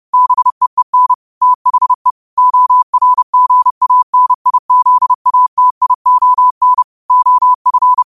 Morse code, heard as a keyed tone, about 30 wpm, 1000 Hz. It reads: BEEN THE ORGANIZATION OF
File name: qotdfiles / 30wpm